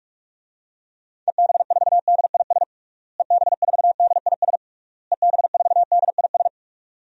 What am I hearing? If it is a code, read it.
Morse code, heard as a keyed tone, about 46 wpm, 700 Hz. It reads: EB4DIS EB4DIS EB4DIS
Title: DXCC-100 Call Signs - 6 Characters - Repeated 3x - 45wpm